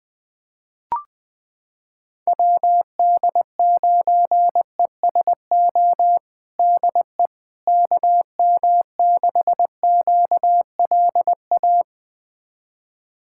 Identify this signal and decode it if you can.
Morse code, heard as a keyed tone, about 20 wpm, 700 Hz. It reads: WD9ESO DE KM6QLA
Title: USA Callsign DE Callsign 20wpm